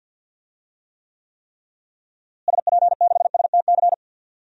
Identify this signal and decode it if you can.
Morse code, heard as a keyed tone, about 51 wpm, 700 Hz. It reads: SP6STC
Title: DXCC-100 Call Signs - 6 Characters - Repeated 3x - 50wpm